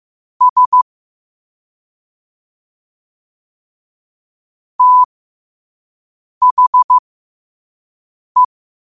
Morse code, heard as a keyed tone, about 15 wpm, spaced as for 5 wpm, 1000 Hz. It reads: S THE
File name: 5wpm